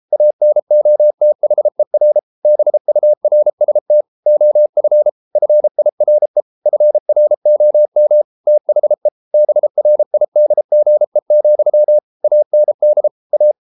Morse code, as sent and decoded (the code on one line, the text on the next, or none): .- -. --- - .... .-. / -... ..- .-. ... - / --- ..-. / ..-. .. .-. / ..-. .-. --- -- / - .... / -... .-. .. -.. --. --..-- / .- -. -.. / .-
ANOTHER BURST OF FIRE FROM THE BRIDGE, AND A